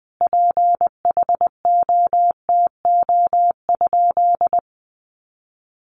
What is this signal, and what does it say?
Morse code, heard as a keyed tone, about 20 wpm, 700 Hz. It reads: PHOTO?
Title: Sentences from Top 1200 Words 20wpm